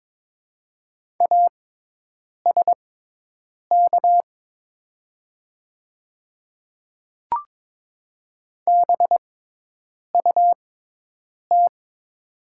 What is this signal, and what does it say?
Morse code, heard as a keyed tone, about 22 wpm, spaced as for 8 wpm, 700 Hz. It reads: ASK BUT
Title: 6x Character Spacing - Top 200 Words - 22wpm